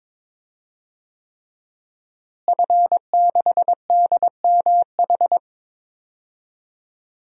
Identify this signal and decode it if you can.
Morse code, heard as a keyed tone, about 22 wpm, 700 Hz. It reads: F6DMH
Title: DXCC-100 Call Signs - 5 Characters - Repeated 3x - 22wpm